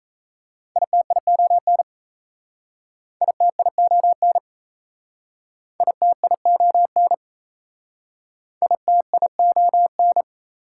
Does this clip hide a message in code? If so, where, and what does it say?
Yes: Morse code from 0.8 to 10.2 s, heard as a keyed tone, about 38 wpm, 690 Hz. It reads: ITION ITION ITION ITION